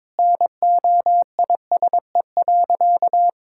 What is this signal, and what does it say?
Morse code, heard as a keyed tone, about 22 wpm, 700 Hz. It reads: NOISE.
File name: practice / 22wpm / Sentences with 2-Word Phrases - 22wpm